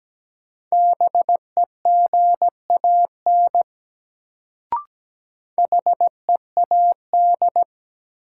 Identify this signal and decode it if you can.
Morse code, heard as a keyed tone, about 17 wpm, 700 Hz. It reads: BEGAN HEAD
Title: Sentences from Top 300 Words - New Word Drill - 17wpm